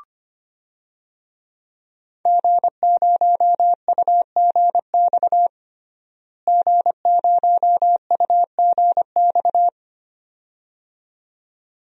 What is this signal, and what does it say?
Morse code, heard as a keyed tone, about 25 wpm, 700 Hz. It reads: G0UGX G0UGX
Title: DXCC-100 Call Signs - 5 Characters - Repeated 2x - 25wpm